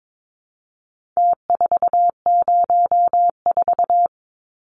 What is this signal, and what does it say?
Morse code, heard as a keyed tone, about 22 wpm, 700 Hz. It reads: T404
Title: Numbers - Contest Style - 4-Digit 22wpm